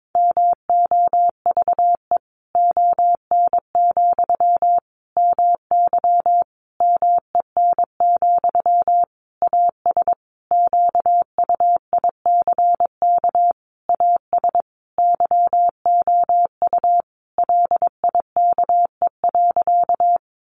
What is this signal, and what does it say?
Morse code, heard as a keyed tone, about 22 wpm, 700 Hz. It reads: MOVE ON, MY MEN, AS QUICK AS YOU LIKE.
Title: Sentences from Top 500 Words 22wpm